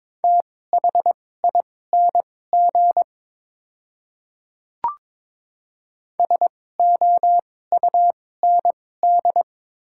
Morse code, heard as a keyed tone, about 22 wpm, spaced as for 16 wpm, 700 Hz. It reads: THING SOUND
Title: 2x Character Spacing - Top 100 Words - 22wpm